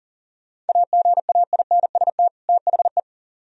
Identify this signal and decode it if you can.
Morse code, heard as a keyed tone, about 51 wpm, 700 Hz. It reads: AGAINST THE